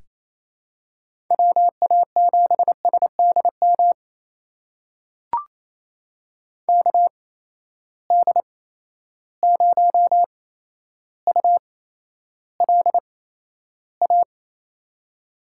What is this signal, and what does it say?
Morse code, heard as a keyed tone, about 28 wpm, spaced as for 12 wpm, 700 Hz. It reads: WA7SDM KD0ULA
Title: USA Call Signs - 6 Characters - 8x Character Spacing - 28wpm